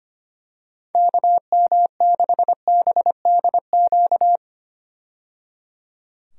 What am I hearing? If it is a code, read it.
Morse code, heard as a keyed tone, about 25 wpm, 700 Hz. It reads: KM6BDQ
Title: USA Call Signs - 6 Characters - 25wpm